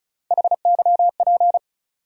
Morse code, heard as a keyed tone, about 34 wpm, 700 Hz. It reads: HYP
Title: USA Call Sign Postfixes - 3 Letters 35wpm